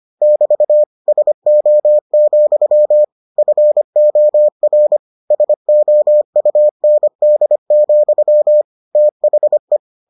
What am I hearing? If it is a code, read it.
Morse code, heard as a keyed tone, about 25 wpm, 600 Hz. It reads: <BT> SO, FOR SOUND, THE